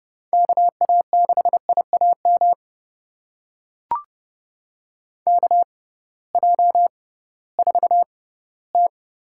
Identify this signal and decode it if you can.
Morse code, heard as a keyed tone, about 30 wpm, spaced as for 17 wpm, 700 Hz. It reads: KA6IAM KJ4T
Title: USA Call Signs - 6 Characters - 6x Character Spacing - 30wpm